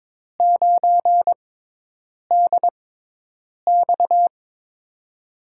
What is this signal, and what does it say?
Morse code, heard as a keyed tone, about 22 wpm, spaced as for 8 wpm, 700 Hz. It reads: 9DX
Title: USA Call Signs - 5 Characters - 6x Character Spacing - 22wpm